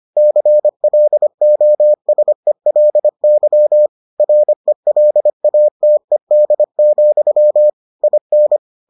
Morse code, heard as a keyed tone, about 25 wpm, 600 Hz. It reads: CLOSELY RELATED, IN